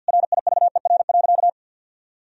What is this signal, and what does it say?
Morse code, heard as a keyed tone, about 50 wpm, 700 Hz. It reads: RIVER.
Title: Sentences from Top 1000 Words - Review Part 4 - 50wpm